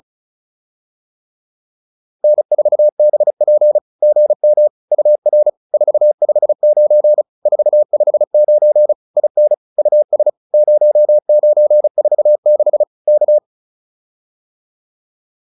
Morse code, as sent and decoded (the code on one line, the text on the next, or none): -. ....- -... .--. / --. -- / ..- .-. / ....- ..... ----. / ....- ..... ----. / .. -. / ..- ... / ----- ----. ....- -.... / -.-
N4BP GM UR 459 459 IN US 0946 K